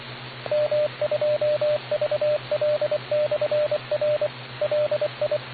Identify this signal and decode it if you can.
Morse code, heard as a keyed tone, about 24 wpm, 600 Hz, with radio noise. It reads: M2VL/R LI